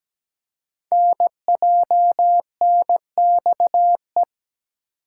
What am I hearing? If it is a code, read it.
Morse code, heard as a keyed tone, about 17 wpm, 700 Hz. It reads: NJNXE
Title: Code Groups - 5 Letters - 17wpm